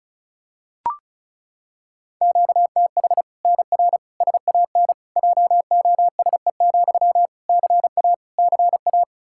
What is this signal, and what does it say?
Morse code, heard as a keyed tone, about 34 wpm, 700 Hz. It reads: QTH NR SAN JOSE, CA CA
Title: Practice QSO - Exchange 1 - Beginning - 35wpm